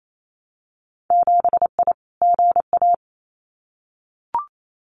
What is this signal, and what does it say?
Morse code, heard as a keyed tone, about 28 wpm, 700 Hz. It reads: 7I GA
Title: Winter Field Day Exchanges Encore 28wpm